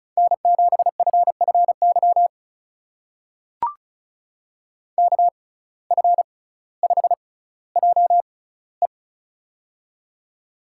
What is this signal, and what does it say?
Morse code, heard as a keyed tone, about 36 wpm, spaced as for 18 wpm, 700 Hz. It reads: N7FFY KF5JE
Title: USA Call Signs - 5 Characters - 6x Character Spacing - 35wpm